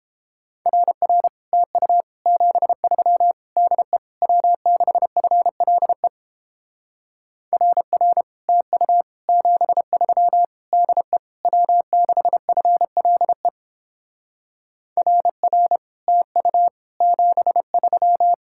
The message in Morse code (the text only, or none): RR TU 73 DE W6FLE RR TU 73 DE W6FLE RR TU 73